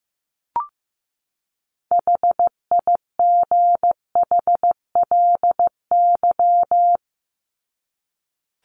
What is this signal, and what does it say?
Morse code, heard as a keyed tone, about 15 wpm, 700 Hz. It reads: HIGHLY
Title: Top 1001-1250 Words 15wpm